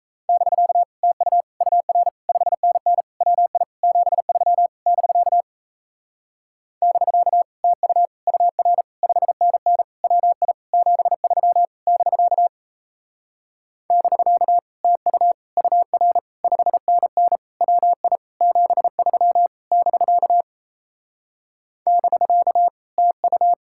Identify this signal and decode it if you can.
Morse code, heard as a keyed tone, about 37 wpm, 700 Hz. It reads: <BK> TU UR 5NN WI 73 <BK> <BK> TU UR 5NN WI 73 <BK> <BK> TU UR 5NN WI 73 <BK> <BK> TU